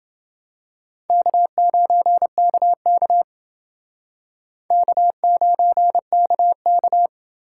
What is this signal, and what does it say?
Morse code, heard as a keyed tone, about 28 wpm, 700 Hz. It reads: K9KK K9KK